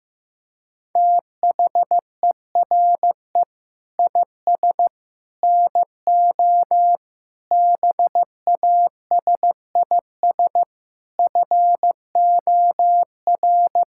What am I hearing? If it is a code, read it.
Morse code, heard as a keyed tone, about 15 wpm, 700 Hz. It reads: THERE IS NO BASIS FOR
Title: Sentences from Top 1600 Words 15wpm